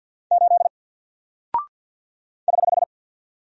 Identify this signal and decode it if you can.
Morse code, heard as a keyed tone, about 48 wpm, 700 Hz. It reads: 8 <HH>
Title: Single Character (Letter:Number:Punctuation:Prosign) 50wpm